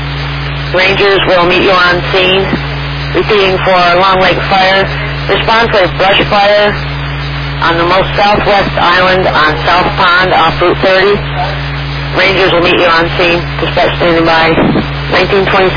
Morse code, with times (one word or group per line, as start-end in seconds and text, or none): none